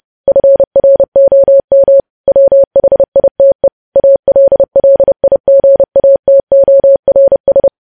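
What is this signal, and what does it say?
Morse code, heard as a keyed tone, about 30 wpm, 550 Hz. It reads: FROM WHITE ALLIGATORS
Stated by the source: Morse code practice of news headlines sent at 30 wpm.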